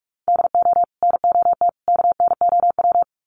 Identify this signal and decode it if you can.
Morse code, heard as a keyed tone, about 44 wpm, 700 Hz. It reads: DO NOT KNOW
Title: Sets of 3 Words 45wpm